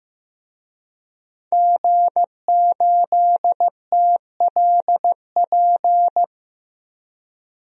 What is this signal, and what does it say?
Morse code, heard as a keyed tone, about 16 wpm, 700 Hz. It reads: G8TLP
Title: DXCC-100 Call Signs - 5 Characters - 15wpm